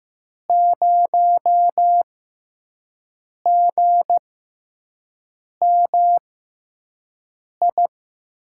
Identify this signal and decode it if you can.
Morse code, heard as a keyed tone, about 15 wpm, spaced as for 5 wpm, 700 Hz. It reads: 0GMI
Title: USA Call Signs - 5 Characters - 6x Character Spacing - 15wpm